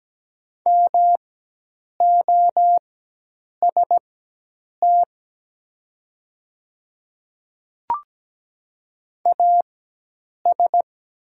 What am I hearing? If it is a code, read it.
Morse code, heard as a keyed tone, about 17 wpm, spaced as for 8 wpm, 700 Hz. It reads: MOST AS
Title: 4x Character Spacing - Top 100 Words - 17wpm